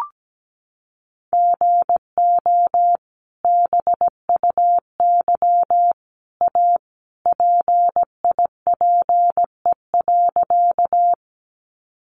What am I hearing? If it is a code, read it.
Morse code, heard as a keyed tone, about 17 wpm, 700 Hz. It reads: GO BUY A PIPE.
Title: Sentences from Top 1500 Words 17wpm